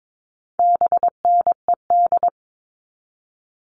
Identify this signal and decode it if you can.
Morse code, heard as a keyed tone, about 22 wpm, 700 Hz. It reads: BNED
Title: All Cut Numbers - 4-Digit 22wpm